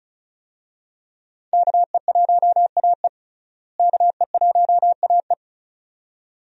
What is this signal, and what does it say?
Morse code, heard as a keyed tone, about 34 wpm, 700 Hz. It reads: KE1AE KE1AE
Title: USA Call Signs - 5 Characters - Repeated 2x - 35wpm